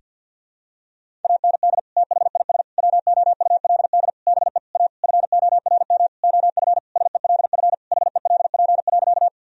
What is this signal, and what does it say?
Morse code, heard as a keyed tone, about 48 wpm, 700 Hz. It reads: AND THIS WOULD BE A FORM OF SELF HELP.
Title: Sentences from Top 1000 Words - Review Part 1 - 50wpm